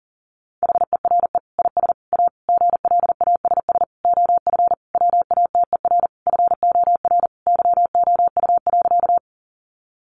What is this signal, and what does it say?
Morse code, heard as a keyed tone, about 37 wpm, 700 Hz. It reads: HERE IS A GLASS OF WATER FOR YOU.
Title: Sentences from Top 1300 Words 40wpm